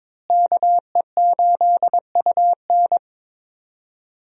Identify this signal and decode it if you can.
Morse code, heard as a keyed tone, about 22 wpm, 700 Hz. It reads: KE8UN
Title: USA Call Signs - 5 Characters - Repeated 2x - 22wpm